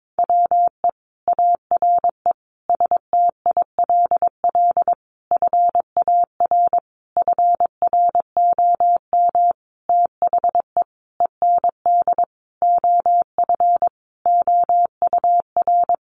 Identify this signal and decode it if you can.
Morse code, heard as a keyed tone, about 22 wpm, 700 Hz. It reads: WE ARE STILL FAR FROM THE END OF OUR
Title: Sentences from Top 1300 Words 22wpm